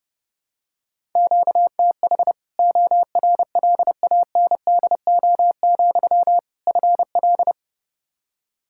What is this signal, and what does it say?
Morse code, heard as a keyed tone, about 30 wpm, 700 Hz. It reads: QTH ORLANDO, FL
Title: Practice QSO - Exchange 1 - Beginning - 30wpm